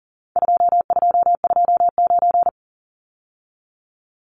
Morse code, heard as a keyed tone, about 40 wpm, 700 Hz. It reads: 2229
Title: Numbers - 4-Digit 40wpm